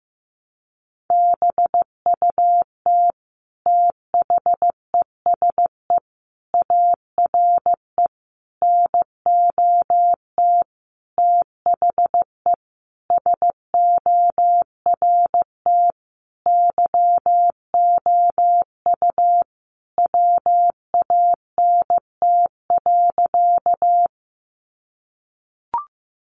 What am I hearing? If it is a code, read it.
Morse code, heard as a keyed tone, about 15 wpm, 700 Hz. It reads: BUT THESE ARE NOT THE SORT YOU WANT.
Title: Sentences from Top 1300 Words 15wpm